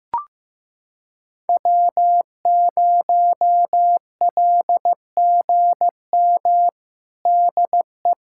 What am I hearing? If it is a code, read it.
Morse code, heard as a keyed tone, about 16 wpm, 700 Hz. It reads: W0LGM DE
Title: USA Callsign DE Callsign - 5 Characters - 15wpm